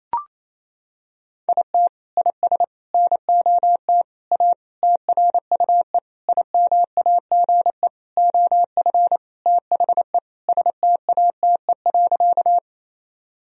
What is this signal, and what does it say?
Morse code, heard as a keyed tone, about 28 wpm, 700 Hz. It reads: IT IS NOT A TRUE IMAGE OF THE STATE.
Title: Sentences from Top 1000 Words - Review Part 4 - 28wpm